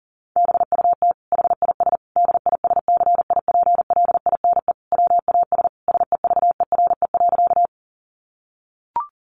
Morse code, heard as a keyed tone, about 40 wpm, 700 Hz. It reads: BUT HIS DISCIPLINE WAS SEVERE.
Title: Sentences from Top 1400 Words 40wpm